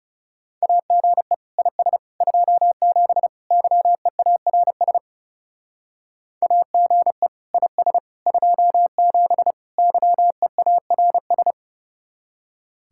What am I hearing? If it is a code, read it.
Morse code, heard as a keyed tone, about 36 wpm, 700 Hz. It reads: AGE IS 27 YEARS AGE IS 27 YEARS
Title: Speed-Racing Common QSO Phrases - 30wpm